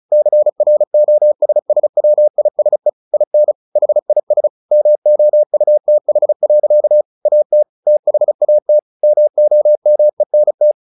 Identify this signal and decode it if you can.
Morse code, heard as a keyed tone, about 34 wpm, 600 Hz. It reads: CROSSWISE IN HIS MOUTH. AT THAT MOMENT
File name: the old man and the sea / the old man and the sea 1 35wpm